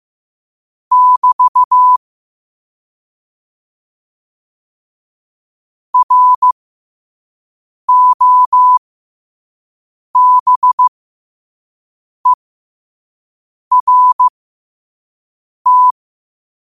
Morse code, heard as a keyed tone, about 15 wpm, spaced as for 5 wpm, 1000 Hz. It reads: <BT> ROBERT
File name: qotdfiles / 5wpm